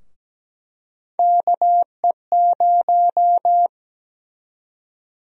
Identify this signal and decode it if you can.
Morse code, heard as a keyed tone, about 17 wpm, 700 Hz. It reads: KE0
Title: USA Call Sign Prefixes - 17wpm